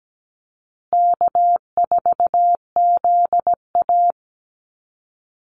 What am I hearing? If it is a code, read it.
Morse code, heard as a keyed tone, about 17 wpm, 700 Hz. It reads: K4ZA
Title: Easy Call Signs Repeated 3x - 17wpm